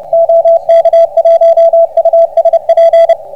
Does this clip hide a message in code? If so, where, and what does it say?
Yes: Morse code from 0.1 to 3.1 s, heard as a keyed tone, about 24 wpm, 660 Hz, with radio noise. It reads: OK1USP